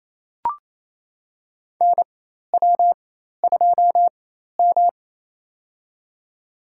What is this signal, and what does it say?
Morse code, heard as a keyed tone, about 28 wpm, spaced as for 13 wpm, 700 Hz. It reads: NW2M
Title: USA Call Signs - 4 Characters - 4x Character Spacing - 28wpm